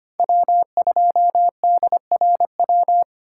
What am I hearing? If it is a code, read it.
Morse code, heard as a keyed tone, about 32 wpm, 700 Hz. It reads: W2DRW